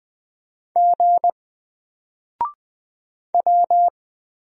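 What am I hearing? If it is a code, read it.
Morse code, heard as a keyed tone, about 20 wpm, 700 Hz. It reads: G W
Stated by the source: Single letter and number Morse drill at 20 wpm, from N4LCD.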